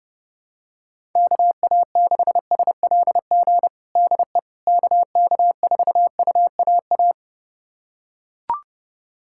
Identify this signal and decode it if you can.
Morse code, heard as a keyed tone, about 30 wpm, 700 Hz. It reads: KA6SLG DE KK4UAA
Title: USA Callsign DE Callsign - 6 Characters - 30wpm